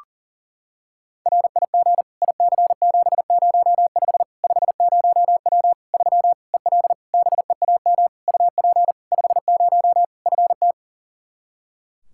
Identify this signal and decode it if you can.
Morse code, heard as a keyed tone, about 40 wpm, 700 Hz. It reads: RIG IC705 50W 3 EL BEAM UP 50 FT
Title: Practice QSO - Exchange 2 - Info - 40wpm